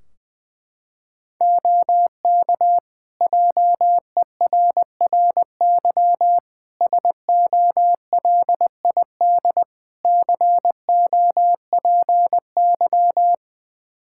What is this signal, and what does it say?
Morse code, heard as a keyed tone, about 21 wpm, 700 Hz. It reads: OK JERRY SOLID COPY
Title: Practice QSO - Exchange 3 - More Info - 20wpm